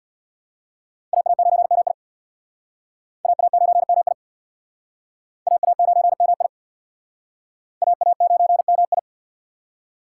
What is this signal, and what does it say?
Morse code, heard as a keyed tone, about 66 wpm, 700 Hz. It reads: AA9MI AA9MI AA9MI AA9MI